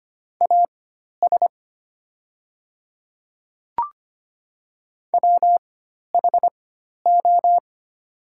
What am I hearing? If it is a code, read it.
Morse code, heard as a keyed tone, about 25 wpm, spaced as for 12 wpm, 700 Hz. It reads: AS WHO
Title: 4x Character Spacing - Top 100 Words - 25wpm